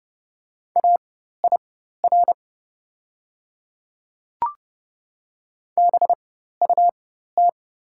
Morse code, heard as a keyed tone, about 30 wpm, spaced as for 14 wpm, 700 Hz. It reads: AIR BUT
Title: 4x Character Spacing - Top 200 Words - 30wpm